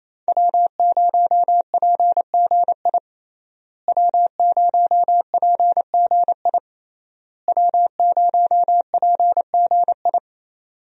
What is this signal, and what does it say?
Morse code, heard as a keyed tone, about 28 wpm, 700 Hz. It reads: W0PGI W0PGI W0PGI
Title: USA Call Signs - 5 Characters - Repeated 3x - 28wpm